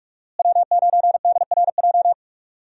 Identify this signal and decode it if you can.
Morse code, heard as a keyed tone, about 46 wpm, 700 Hz. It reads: W9DRJ
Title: USA Call Signs - 5 Characters - Repeated 3x - 45wpm